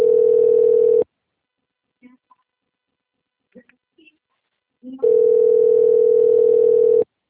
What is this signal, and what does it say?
Ringback tone (North America), 440 + 480 Hz, with speech.
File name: out- - - - - .53923